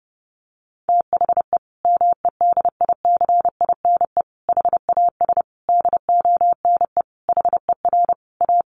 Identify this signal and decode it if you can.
Morse code, heard as a keyed tone, about 30 wpm, 700 Hz. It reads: THE MEDICINE HAS DONE HER A
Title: Sentences from Top 900 Words 30wpm